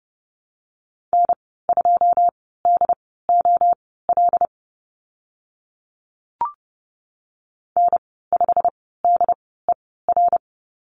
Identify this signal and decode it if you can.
Morse code, heard as a keyed tone, about 30 wpm, spaced as for 17 wpm, 700 Hz. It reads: N2DOL N5DER
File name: Lesson 15g - Number 5 - Call Signs 30wpm